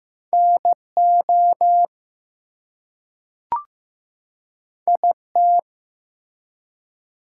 Morse code, heard as a keyed tone, about 15 wpm, 700 Hz. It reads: NO IT
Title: Two Letter Words - 15wpm